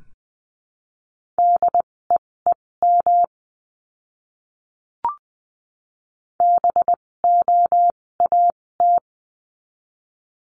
Morse code, heard as a keyed tone, about 20 wpm, spaced as for 16 wpm, 700 Hz. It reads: DEEM BOAT
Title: 256 Lesson 42a - Words 12-20wpm